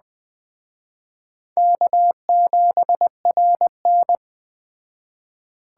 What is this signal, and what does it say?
Morse code, heard as a keyed tone, about 20 wpm, 700 Hz. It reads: K7RN
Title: Speed Racing - USA 1x2 Call Signs 17wpm